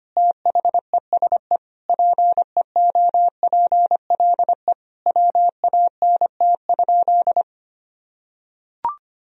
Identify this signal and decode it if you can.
Morse code, heard as a keyed tone, about 25 wpm, 700 Hz. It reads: THESE PEOPLE WANT?
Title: Sentences from Top 1000 Words - Review Part 1 - 25wpm